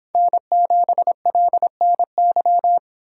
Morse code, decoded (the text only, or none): N7LNY